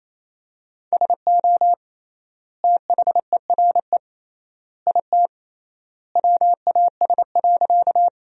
Morse code, heard as a keyed tone, about 28 wpm, 700 Hz. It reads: SO THERE IT WAS.
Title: 3x Word-Spacing - Sentences from Top 100 Words - 28wpm